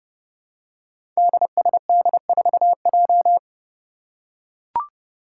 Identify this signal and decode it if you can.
Morse code, heard as a keyed tone, about 30 wpm, 700 Hz. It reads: DSD4J
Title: Five Challenging Letter-Number Combinations 30wpm